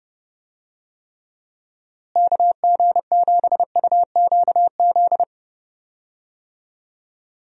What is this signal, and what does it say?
Morse code, heard as a keyed tone, about 31 wpm, 700 Hz. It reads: KG7UQZ
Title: USA Call Signs - 6 Characters - Repeated 2x - 30wpm